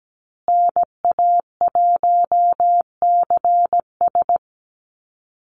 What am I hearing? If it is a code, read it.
Morse code, heard as a keyed tone, about 17 wpm, 700 Hz. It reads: NA1CS
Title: USA Call Signs Repeated 2x 17wpm